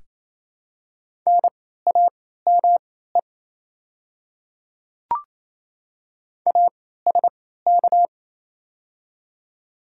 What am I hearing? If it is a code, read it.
Morse code, heard as a keyed tone, about 27 wpm, spaced as for 16 wpm, 700 Hz. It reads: NAME ASK
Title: 3x Character Spacing - Top 200 Words - 28wpm